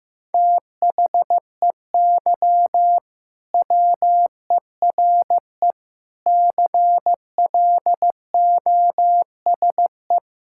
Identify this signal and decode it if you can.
Morse code, heard as a keyed tone, about 15 wpm, 700 Hz. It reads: THEY WERE CLOSE
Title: Sentences from Top 1000 Words - Review Part 6 - 15wpm